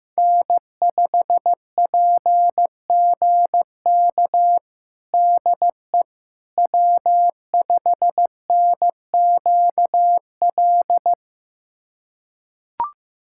Morse code, heard as a keyed tone, about 15 wpm, 700 Hz. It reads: N5PGK DE W5NQL
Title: USA Callsign DE Callsign - 5 Characters - 15wpm